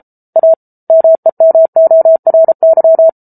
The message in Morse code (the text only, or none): A MEMORY